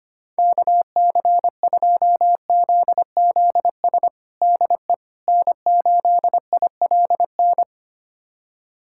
Morse code, heard as a keyed tone, about 25 wpm, 700 Hz. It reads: KC2ZZS DE N8ILN
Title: USA Callsign DE Callsign 25wpm